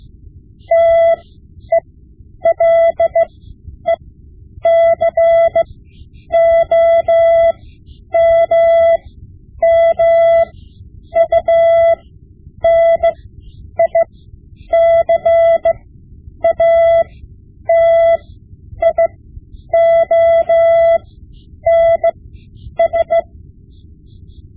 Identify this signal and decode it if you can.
Morse code, heard as a keyed tone, about 16 wpm, spaced as for 9 wpm, 670 Hz. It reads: TELECOMMUNICATIONS
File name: Morse Lessons / Morse Lessons / Tape-5B